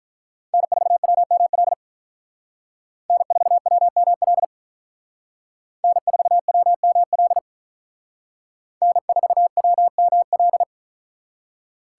Morse code, heard as a keyed tone, about 45 wpm, 700 Hz. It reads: N4WML N4WML N4WML N4WML